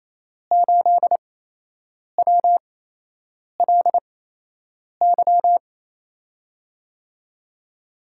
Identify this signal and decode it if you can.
Morse code, heard as a keyed tone, about 28 wpm, spaced as for 8 wpm, 700 Hz. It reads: 8WLY
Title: USA Call Signs - 5 Characters - 8x Character Spacing - 28wpm